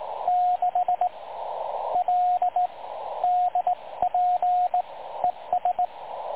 Morse code, heard as a keyed tone, about 19 wpm, 720 Hz, with radio noise. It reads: 6 L DP ES
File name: dlazaawansowanych / nagraniazpasma / websdr recording 3565.0kHz